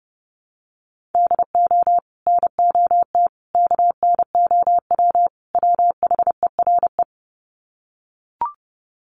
Morse code, heard as a keyed tone, about 30 wpm, 700 Hz. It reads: DO NOT KNOW WHERE
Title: Sets of 4 Words 30wpm